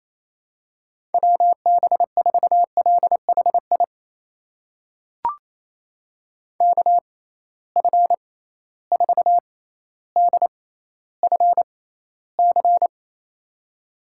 Morse code, heard as a keyed tone, about 28 wpm, spaced as for 14 wpm, 700 Hz. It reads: WB4LHI KF4DFC
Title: USA Call Signs - 6 Characters - 6x Character Spacing - 28wpm